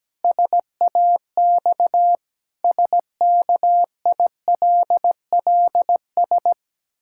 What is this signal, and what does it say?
Morse code, heard as a keyed tone, about 17 wpm, 700 Hz. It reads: SAX SKILLS
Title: Pangram Challenge v2 - 17wpm